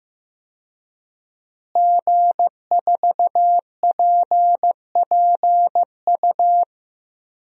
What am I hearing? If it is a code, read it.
Morse code, heard as a keyed tone, about 15 wpm, 700 Hz. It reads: G4PPU
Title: DXCC-100 Call Signs - 5 Characters - Repeated 2x - 15wpm